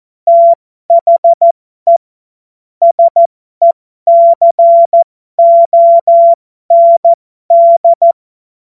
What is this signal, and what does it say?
Morse code, heard as a keyed tone, about 14 wpm, spaced as for 12 wpm, 680 Hz. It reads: THE SECOND
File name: Morsekurs / Witze